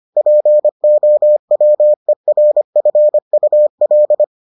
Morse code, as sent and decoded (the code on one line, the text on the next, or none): .--. --- .-- . .-. ..-. ..- .-..
POWERFUL